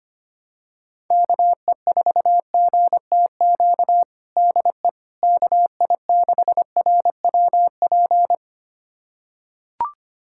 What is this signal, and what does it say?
Morse code, heard as a keyed tone, about 25 wpm, 700 Hz. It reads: KE4GTQ DE KI6RWP
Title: USA Callsign DE Callsign - 6 Characters - 25wpm